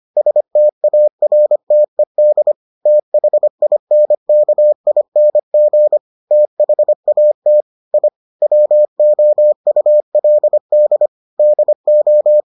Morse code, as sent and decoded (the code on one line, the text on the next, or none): ... - .- .-. - . -.. / - .... .. -. -.- .. -. --. / - .... .- - / .. / .-- --- ..- .-.. -.. / -.. ---
STARTED THINKING THAT I WOULD DO